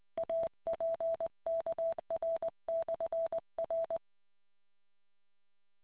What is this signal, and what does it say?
Morse code, heard as a keyed tone, about 20 wpm, 660 Hz. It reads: APKR/R